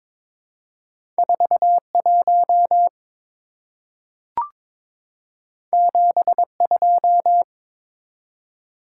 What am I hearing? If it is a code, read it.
Morse code, heard as a keyed tone, about 22 wpm, 700 Hz. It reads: 41 72
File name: practice / 22wpm / Two Letter-Number Combinations - 22wpm